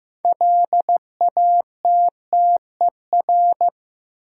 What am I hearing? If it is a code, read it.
Morse code, heard as a keyed tone, about 15 wpm, 700 Hz. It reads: LATTER
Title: Sentences from Top 1600 Words - New Word Drill - 15wpm